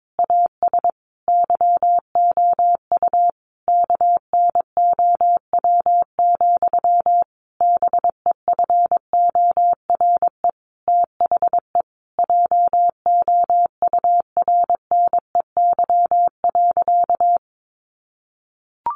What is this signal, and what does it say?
Morse code, heard as a keyed tone, about 22 wpm, 700 Hz. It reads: AS YOU KNOW, BEFORE THE JOURNEY.
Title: A Sound Of Thunder 22wpm